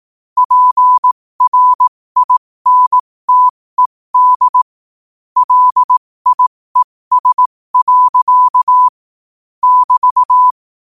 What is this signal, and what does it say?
Morse code, heard as a keyed tone, about 18 wpm, spaced as for 15 wpm, 1000 Hz. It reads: PRINTED LIES. <BT>